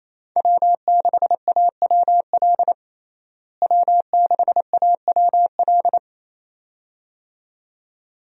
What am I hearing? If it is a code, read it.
Morse code, heard as a keyed tone, about 28 wpm, 700 Hz. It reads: W6AWL W6AWL
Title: USA Call Signs - 5 Characters - Repeated 2x - 28wpm